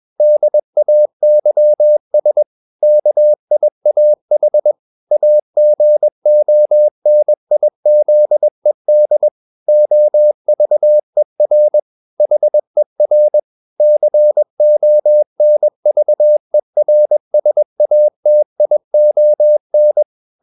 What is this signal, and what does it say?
Morse code, heard as a keyed tone, about 21 wpm, 600 Hz. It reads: DAYS KIAH AGONIZED OVER HER CONVERSATION
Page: CW med Gnister